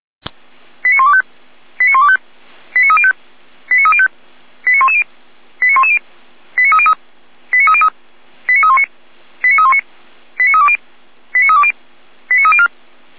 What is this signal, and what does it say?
Five-tone selective call (ZVEI), heard as quick runs of five tones, tones of about 70 ms. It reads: 89125 89125 89305 89305 89110 89110 89303 89303 89319 89319 89320 89320 89304
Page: SDR/ZVEI